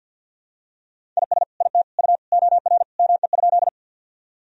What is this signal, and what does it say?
Morse code, heard as a keyed tone, about 48 wpm, 700 Hz. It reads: IS IT U OR ME?
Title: Sentences from Top 200 Words and Abbreviations 50wpm